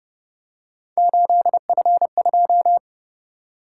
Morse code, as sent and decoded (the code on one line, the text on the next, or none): ---.. ..-. ..---
8F2